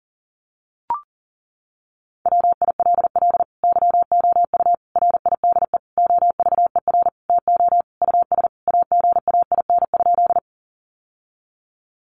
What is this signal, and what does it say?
Morse code, heard as a keyed tone, about 40 wpm, 700 Hz. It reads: WILL YOU RIDE OVER TO US AGAIN?
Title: Sentences from Top 1200 Words 40wpm